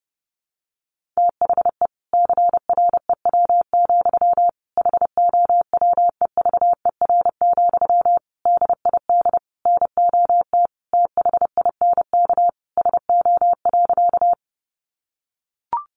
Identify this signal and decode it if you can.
Morse code, heard as a keyed tone, about 30 wpm, 700 Hz. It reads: THE CREW, HOWEVER, DID NOT THINK SO.
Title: Sentences from Top 1500 Words 30wpm